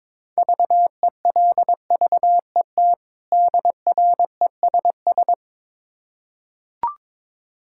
Morse code, heard as a keyed tone, about 22 wpm, 690 Hz. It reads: VELVET DRESS
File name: Difficult Words with Hints - 22wpm